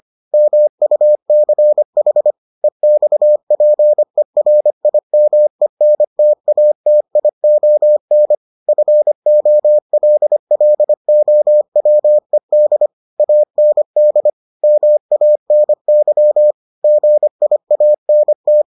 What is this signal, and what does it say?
Morse code, heard as a keyed tone, about 25 wpm, 600 Hz. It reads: MUCH EXPERIMENTATION FOLLOWED AND MANY GIANT